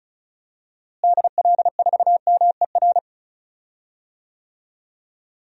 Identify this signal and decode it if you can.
Morse code, heard as a keyed tone, about 36 wpm, 700 Hz. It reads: DL4MER
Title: DXCC-100 Call Signs - 6 Characters - Repeated 2x - 35wpm